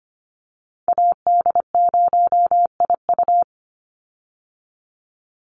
Morse code, heard as a keyed tone, about 25 wpm, 700 Hz. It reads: AD0IU
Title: Call Signs - 25wpm